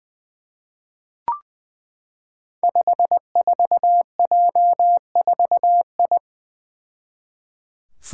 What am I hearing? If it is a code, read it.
Morse code, heard as a keyed tone, about 20 wpm, 700 Hz. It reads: 54J4I
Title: Five Challenging Letter-Number Combinations 20wpm